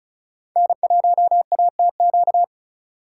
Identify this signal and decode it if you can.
Morse code, heard as a keyed tone, about 38 wpm, 700 Hz. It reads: N1ATQ